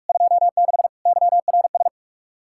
Morse code, heard as a keyed tone, about 59 wpm, spaced as for 52 wpm, 700 Hz. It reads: 26 YRS
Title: Speed-Racing Common QSO Phrases - 45wpm